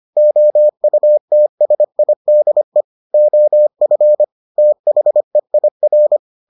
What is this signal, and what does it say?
Morse code, heard as a keyed tone, about 25 wpm, 600 Hz. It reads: OUTSIDE OF THEIR